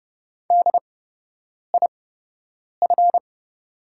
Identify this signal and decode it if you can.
Morse code, heard as a keyed tone, about 30 wpm, spaced as for 8 wpm, 700 Hz. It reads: DIF
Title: 8x Character Spacing - Top 200 Words - 30wpm